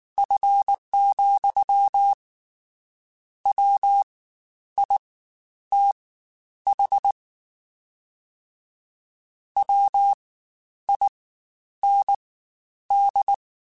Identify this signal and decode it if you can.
Morse code, heard as a keyed tone, about 19 wpm, spaced as for 10 wpm, 800 Hz. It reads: F, WITH WIND
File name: Spaced Characters / Spaced Characters-mp3s / Weather5 5 sp @18